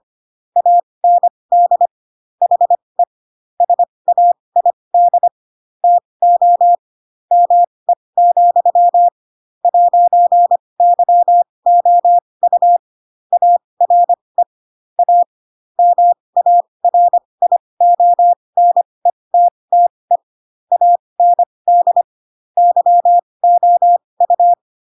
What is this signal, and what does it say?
Morse code, heard as a keyed tone, about 25 wpm, spaced as for 20 wpm, 700 Hz. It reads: AND HE SAID TO ME, 'YOU ARE A MARIONETTE AND YOU
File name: Pinocchio / Pinocchio ch7 15 25 ik1hge